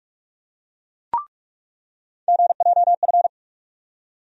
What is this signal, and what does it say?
Morse code, heard as a keyed tone, about 46 wpm, 700 Hz. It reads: GJF
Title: USA Call Sign Postfixes - 3 Letters 45wpm